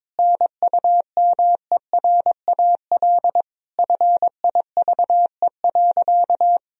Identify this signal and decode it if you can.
Morse code, heard as a keyed tone, about 22 wpm, 700 Hz. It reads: NUMERAL FIVE.
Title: Sentences from Top 1000 Words - Review Part 2 - 22wpm